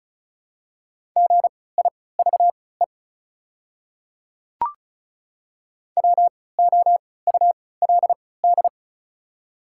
Morse code, heard as a keyed tone, about 36 wpm, spaced as for 20 wpm, 700 Hz. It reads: GIVE WOULD
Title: 3x Character Spacing - Top 200 Words - 35wpm